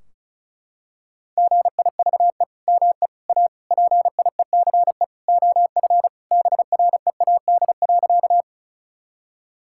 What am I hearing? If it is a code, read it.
Morse code, heard as a keyed tone, about 34 wpm, 700 Hz. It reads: GIVE ME A PIECE OF BREAD.
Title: Sentences from Top 1600 Words 35wpm